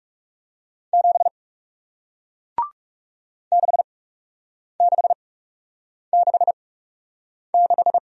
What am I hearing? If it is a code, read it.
Morse code, heard as a keyed tone, about 38 wpm, spaced as for 8 wpm, 700 Hz. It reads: 7 6666